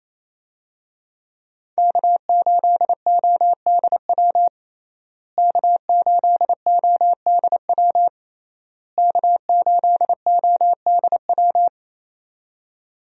Morse code, heard as a keyed tone, about 27 wpm, 700 Hz. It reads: K8ODW K8ODW K8ODW
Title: USA Call Signs - 5 Characters - Repeated 3x - 28wpm